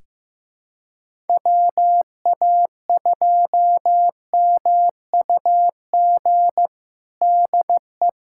Morse code, heard as a keyed tone, about 16 wpm, 700 Hz. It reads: WA2MUG DE
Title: USA Callsign DE Callsign - 6 Characters - 15wpm